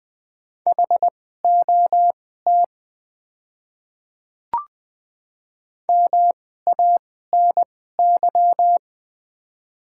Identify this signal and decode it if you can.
Morse code, heard as a keyed tone, about 20 wpm, spaced as for 15 wpm, 690 Hz. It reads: HOT MANY
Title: 2x Character Spacing - Top 300 Words - 20wpm